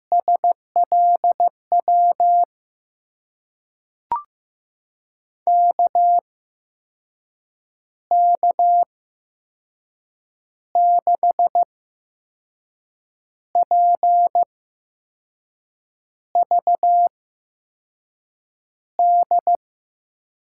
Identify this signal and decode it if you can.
Morse code, heard as a keyed tone, about 15 wpm, spaced as for 5 wpm, 700 Hz. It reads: SLW KK6PVD
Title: USA Call Signs - 6 Characters - 8x Character Spacing - 15wpm